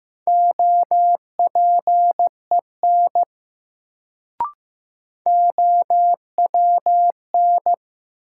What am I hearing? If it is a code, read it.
Morse code, heard as a keyed tone, about 15 wpm, 700 Hz. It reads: OPEN OWN
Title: Sentences from Top 300 Words - New Word Drill - 15wpm